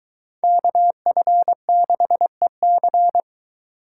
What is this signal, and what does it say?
Morse code, heard as a keyed tone, about 22 wpm, 700 Hz. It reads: KF6EC